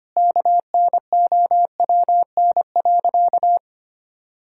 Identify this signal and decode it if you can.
Morse code, heard as a keyed tone, about 25 wpm, 700 Hz. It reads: KNOWN.
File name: practice / 25wpm / Sentences with 3-Word Phrases - 25wpm